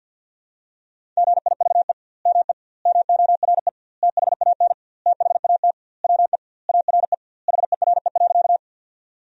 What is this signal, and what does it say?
Morse code, heard as a keyed tone, about 50 wpm, 700 Hz. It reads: GIVE ME MORE THAN THAT WE ARE HERE.
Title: 2x Word-Spacing - Sentences from Top 200 Words - 50wpm